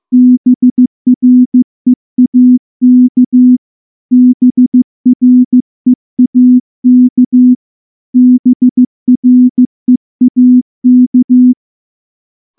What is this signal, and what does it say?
Morse code, heard as a keyed tone, about 15 wpm, 250 Hz. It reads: BREAK BREAK BREAK
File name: morse-training-15WPM